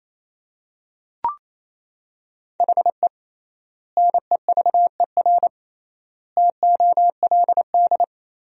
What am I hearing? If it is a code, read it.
Morse code, heard as a keyed tone, about 28 wpm, 700 Hz. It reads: HE NEVER TOLD
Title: 3x Word-Spacing - Sentences from Top 400 Words - 28wpm